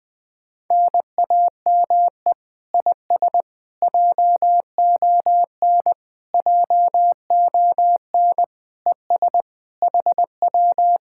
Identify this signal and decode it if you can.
Morse code, heard as a keyed tone, about 20 wpm, 700 Hz. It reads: NAME IS JON JON ES HW